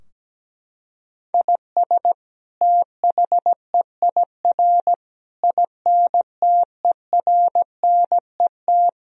Morse code, heard as a keyed tone, about 17 wpm, 700 Hz. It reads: IS THEIR INTERNET
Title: Sentences from Top 1000 Words - Review Part 4 - 17wpm